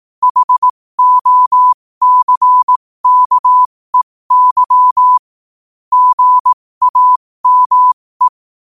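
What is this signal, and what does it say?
Morse code, heard as a keyed tone, about 18 wpm, spaced as for 15 wpm, 1000 Hz. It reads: HOCKEY GAME
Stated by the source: Morse practice sent at 15 wpm.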